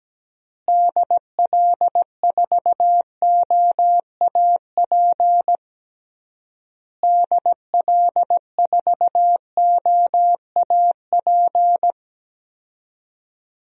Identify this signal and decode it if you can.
Morse code, heard as a keyed tone, about 17 wpm, 700 Hz. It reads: DL4OAP DL4OAP
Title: USA Call Signs - 6 Characters - Repeated 3x - 17wpm